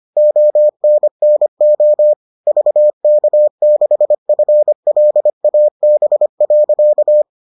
Morse code, heard as a keyed tone, about 25 wpm, 600 Hz. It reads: ONNO VK6FLAB.